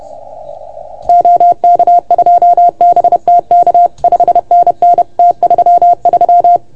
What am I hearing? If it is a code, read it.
Morse code, heard as a keyed tone, about 30 wpm, 690 Hz. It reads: OK2BTK 5NN T33
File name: Archiv / CUC381 / OK1FGD CUC 381